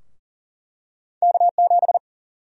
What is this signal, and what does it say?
Morse code, heard as a keyed tone, about 40 wpm, 700 Hz. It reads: K7
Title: USA Call Sign Prefixes - 40wpm